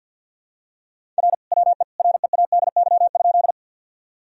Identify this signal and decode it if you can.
Morse code, heard as a keyed tone, about 50 wpm, 700 Hz. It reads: R WE READY?
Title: Sentences from Top 400 Words and Abbreviations 50wpm